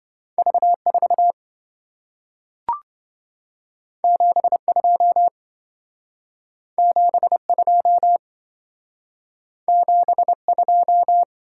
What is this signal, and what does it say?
Morse code, heard as a keyed tone, about 27 wpm, 700 Hz. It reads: V4 72 72 72